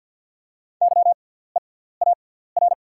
Morse code, heard as a keyed tone, about 48 wpm, spaced as for 17 wpm, 690 Hz. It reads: YEAR
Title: 6x Character Spacing - Top 200 Words - 50wpm